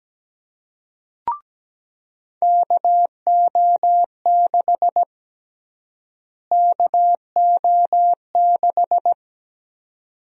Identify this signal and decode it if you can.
Morse code, heard as a keyed tone, about 17 wpm, 700 Hz. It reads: KO6 KO6
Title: USA Call Sign Prefixes - Repeated 2x - 17wpm